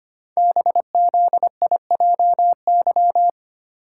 Morse code, heard as a keyed tone, about 25 wpm, 700 Hz. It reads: BZIJY
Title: Code Groups - 5 Letters - 25wpm